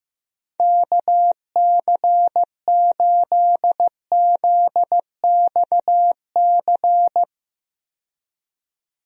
Morse code, heard as a keyed tone, about 15 wpm, 700 Hz. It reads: KC8ZXC
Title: USA Call Signs - 6 Characters - 15wpm